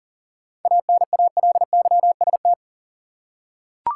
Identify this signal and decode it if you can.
Morse code, heard as a keyed tone, about 40 wpm, 700 Hz. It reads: ANALYST
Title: Seven Letter Words - 40wpm